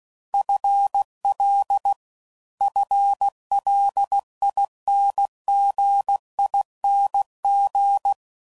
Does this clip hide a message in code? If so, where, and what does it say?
Yes: Morse code from 0.3 to 8.1 s, heard as a keyed tone, about 16 wpm, 800 Hz. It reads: FL FLINGING